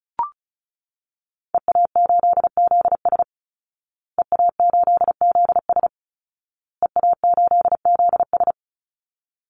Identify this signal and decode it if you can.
Morse code, heard as a keyed tone, about 34 wpm, 700 Hz. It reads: EA8ZS EA8ZS EA8ZS
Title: Call Signs Repeated 2x - 35wpm